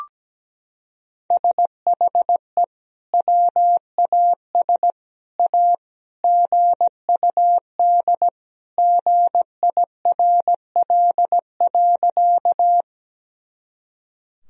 Morse code, heard as a keyed tone, about 17 wpm, 700 Hz. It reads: SHE WAS A GUD GIRL.
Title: Sentences from Top 400 Words and Abbreviations 17wpm